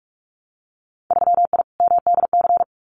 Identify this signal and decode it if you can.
Morse code, heard as a keyed tone, about 46 wpm, 700 Hz. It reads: 3I MDC
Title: Winter Field Day Exchanges Encore 45wpm